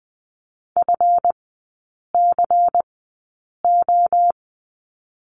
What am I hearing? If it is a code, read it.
Morse code, heard as a keyed tone, about 20 wpm, spaced as for 8 wpm, 700 Hz. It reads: FCO
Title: 178 Lesson 29e - Slash - Three Characters 20wpm